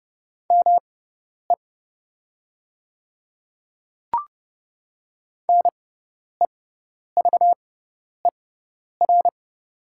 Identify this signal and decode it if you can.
Morse code, heard as a keyed tone, about 32 wpm, spaced as for 10 wpm, 700 Hz. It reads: ME NEVER
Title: 6x Character Spacing - Top 300 Words - 30wpm